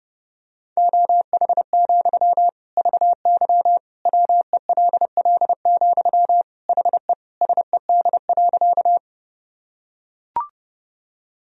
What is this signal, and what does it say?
Morse code, heard as a keyed tone, about 30 wpm, 700 Hz. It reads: OH, VY WELL, HE SED.
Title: Sentences from Top 500 Words and Abbreviations 30wpm